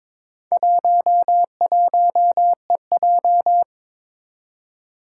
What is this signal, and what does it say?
Morse code, heard as a keyed tone, about 22 wpm, 700 Hz. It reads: 11EJ
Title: Four Challenging Letter-Number Combinations 22wpm